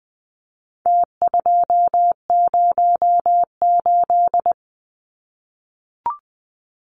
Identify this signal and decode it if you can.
Morse code, heard as a keyed tone, about 20 wpm, 700 Hz. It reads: T208
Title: Numbers - Contest Style - 4-Digit 20wpm